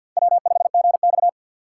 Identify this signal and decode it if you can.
Morse code, heard as a keyed tone, about 47 wpm, 700 Hz. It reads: W5GX
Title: USA Call Signs - 4 Characters - 2x Character Spacing - 50wpm